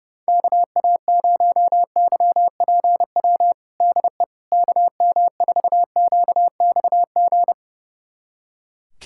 Morse code, heard as a keyed tone, about 30 wpm, 690 Hz. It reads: KA0YPW DE KM4QXG
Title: USA Callsign DE Callsign - 6 Characters - 30wpm